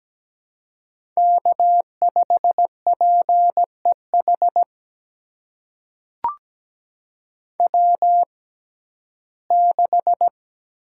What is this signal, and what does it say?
Morse code, heard as a keyed tone, about 17 wpm, spaced as for 12 wpm, 700 Hz. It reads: K5PEH W6
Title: USA Call Signs - 5 Characters - 6x Character Spacing - 17wpm